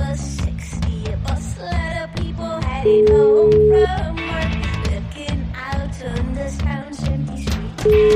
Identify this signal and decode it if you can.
Ringback tone (Europe, 425 Hz), with noise.